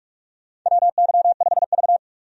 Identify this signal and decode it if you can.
Morse code, heard as a keyed tone, about 46 wpm, 700 Hz. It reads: WY5V
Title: USA Call Signs - 4 Characters - 45wpm